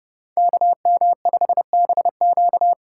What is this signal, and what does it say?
Morse code, heard as a keyed tone, about 31 wpm, 700 Hz. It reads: KM5BQ
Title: USA Call Signs - 5 Characters - 3x Character Spacing - 30wpm